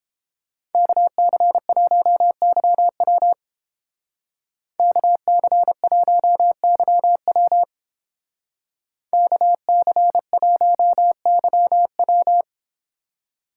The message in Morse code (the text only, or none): KC1YW KC1YW KC1YW